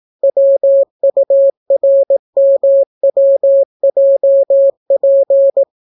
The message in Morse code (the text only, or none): WURMWJP